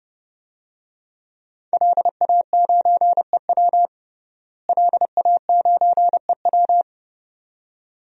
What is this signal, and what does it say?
Morse code, heard as a keyed tone, about 32 wpm, 700 Hz. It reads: LA9EW LA9EW
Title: DXCC-100 Call Signs - 5 Characters - Repeated 2x - 30wpm